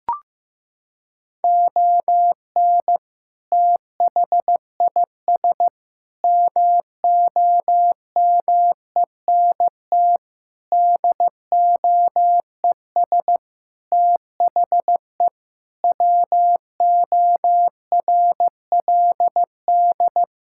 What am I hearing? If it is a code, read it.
Morse code, heard as a keyed tone, about 15 wpm, 700 Hz. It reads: ON THIS MOMENT DOES THE WORLD
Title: Sentences from Top 1600 Words 15wpm